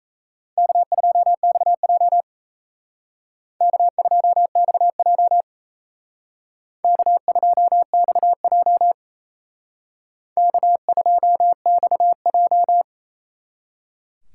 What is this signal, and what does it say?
Morse code, heard as a keyed tone, about 35 wpm, 700 Hz. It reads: K2XJ K2XJ K2XJ K2XJ
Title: Speed Racing - USA 1x2 Call Signs 28wpm